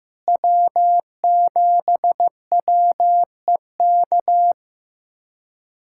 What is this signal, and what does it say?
Morse code, heard as a keyed tone, about 16 wpm, 700 Hz. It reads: W7WEK
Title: USA Callsign DE Callsign - 5 Characters - 15wpm